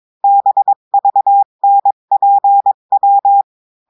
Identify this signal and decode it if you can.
Morse code, heard as a keyed tone, about 22 wpm, 800 Hz. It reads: BVNPW
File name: cours-du-mercredi / audios / cours-853-du-2-mai-2024-etxte chiffre et texte clair-a-20-wpm